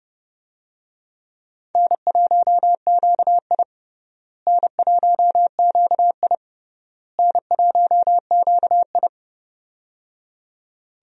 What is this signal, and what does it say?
Morse code, heard as a keyed tone, about 31 wpm, 700 Hz. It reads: N1QI N1QI N1QI
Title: USA Call Signs - 4 Characters - Repeated 3x - 30wpm